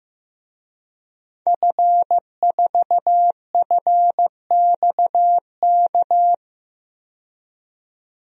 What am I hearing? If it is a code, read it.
Morse code, heard as a keyed tone, about 15 wpm, 700 Hz. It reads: F4FXK
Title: DXCC-100 Call Signs - 5 Characters - 15wpm